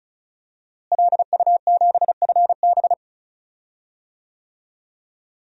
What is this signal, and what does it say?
Morse code, heard as a keyed tone, about 36 wpm, 700 Hz. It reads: LU7FB
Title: DXCC-100 Call Signs - 5 Characters - Repeated 3x - 35wpm